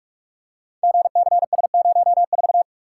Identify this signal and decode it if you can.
Morse code, heard as a keyed tone, about 47 wpm, 700 Hz. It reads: GCS04
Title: Code Groups - 5 Letter:Numbers - 45wpm